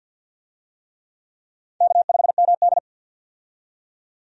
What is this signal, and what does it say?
Morse code, heard as a keyed tone, about 51 wpm, 700 Hz. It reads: K5MD
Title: USA Call Signs - 4 Characters - 50wpm